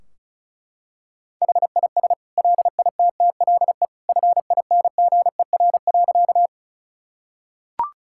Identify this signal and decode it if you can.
Morse code, heard as a keyed tone, about 34 wpm, 700 Hz. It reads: HIS LITTLE FINGER.
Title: Sentences from Top 1600 Words 35wpm